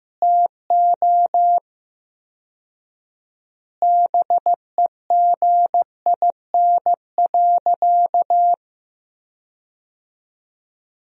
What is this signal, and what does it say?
Morse code, heard as a keyed tone, about 15 wpm, 700 Hz. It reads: TO BEGIN.
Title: 4x Word-Spacing - Sentences from Top 300 Words - 15wpm